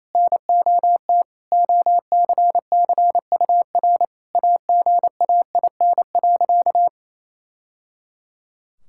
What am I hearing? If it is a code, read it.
Morse code, heard as a keyed tone, about 28 wpm, 700 Hz. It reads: NOT OCCUR AGAIN.
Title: Sentences from Top 1000 Words - Review Part 6 - 28wpm